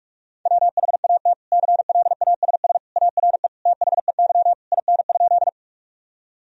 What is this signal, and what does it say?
Morse code, heard as a keyed tone, about 44 wpm, 700 Hz. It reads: WHAT CLASS ARE THEY IN?
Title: Sentences from Top 1000 Words - Review Part 2 - 45wpm